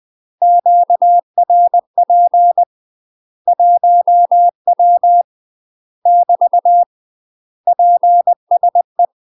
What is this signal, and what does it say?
Morse code, heard as a keyed tone, about 20 wpm, 700 Hz. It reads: QRP 1W <BT> PSE
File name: UK MASTER QSOs TRAINING / 20WPM / UK MASTER QSOs 0059